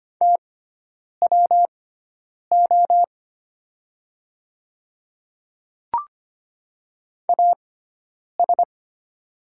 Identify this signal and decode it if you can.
Morse code, heard as a keyed tone, about 25 wpm, spaced as for 9 wpm, 690 Hz. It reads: TWO AS
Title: 6x Character Spacing - Top 100 Words - 25wpm